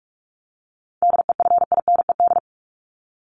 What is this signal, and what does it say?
Morse code, heard as a keyed tone, about 44 wpm, 700 Hz. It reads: DEFINED
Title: ED Suffix - Top 100 Words with Base Words 45wpm